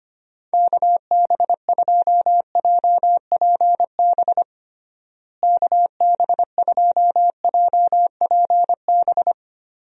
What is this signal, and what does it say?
Morse code, heard as a keyed tone, about 25 wpm, 700 Hz. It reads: KB2JPB KB2JPB
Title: USA Call Signs - 6 Characters - Repeated 3x - 25wpm